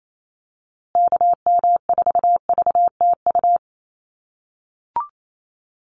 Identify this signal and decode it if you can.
Morse code, heard as a keyed tone, about 28 wpm, 700 Hz. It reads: KM4VTU
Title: USA Call Signs Repeated 2x 28wpm